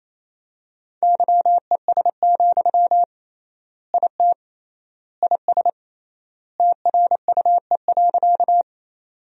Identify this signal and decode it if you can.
Morse code, heard as a keyed tone, about 27 wpm, 700 Hz. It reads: YES, IT IS TRUE.
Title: 3x Word-Spacing - Sentences from Top 500 Words - 28wpm